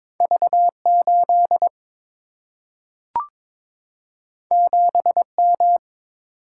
Morse code, heard as a keyed tone, about 22 wpm, 700 Hz. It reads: V 8 7 M
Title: Single-Letter Single-Number 22wpm